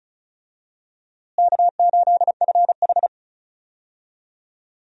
Morse code, heard as a keyed tone, about 36 wpm, 700 Hz. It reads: K8FH
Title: USA Call Signs - 4 Characters - Repeated 3x - 35wpm